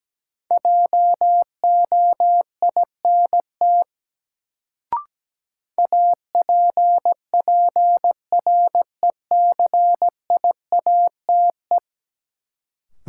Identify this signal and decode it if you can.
Morse code, heard as a keyed tone, about 17 wpm, 700 Hz. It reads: JOINT APPRECIATE
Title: Sentences from Top 1300 Words - New Word Drill Alt - 17wpm